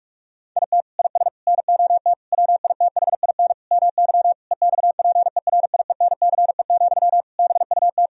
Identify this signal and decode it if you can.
Morse code, heard as a keyed tone, about 45 wpm, 700 Hz. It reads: IT IS NOT WITHIN MY EXPERIENCE, BUT
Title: Sentences from Top 1000 Words - Review Part 4 - 45wpm